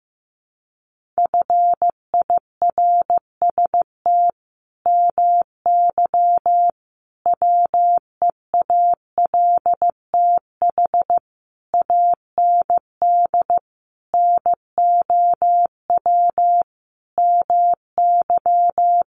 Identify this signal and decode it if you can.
Morse code, heard as a keyed tone, about 15 wpm, 700 Hz. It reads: FIRST MY WEALTH AND NOW MY
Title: Sentences from Top 1200 Words 15wpm